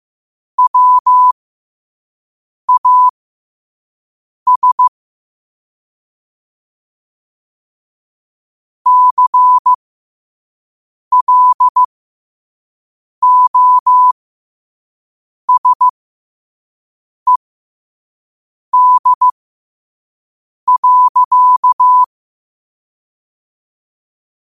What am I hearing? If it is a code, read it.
Morse code, heard as a keyed tone, about 15 wpm, spaced as for 5 wpm, 1000 Hz. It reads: WAS CLOSED.
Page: New quotes every day in morse code at 5 Words per minute.